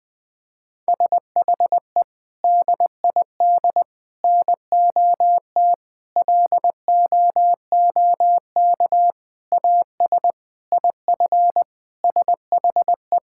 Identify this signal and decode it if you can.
Morse code, heard as a keyed tone, about 20 wpm, 700 Hz. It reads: SHE DID NOT LOOK AS IF SHE
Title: Sentences from Top 1000 Words - Review Part 6 - 20wpm